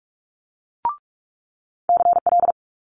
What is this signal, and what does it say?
Morse code, heard as a keyed tone, about 46 wpm, 700 Hz. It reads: CL
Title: Multi-letter Phonemes Quick Fire 45wpm